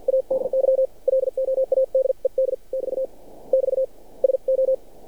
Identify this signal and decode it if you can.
Morse code, heard as a keyed tone, about 50 wpm, spaced as for 43 wpm, 530 Hz. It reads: ADY LOADED <BT> <BT> SO
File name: archived / CRSnet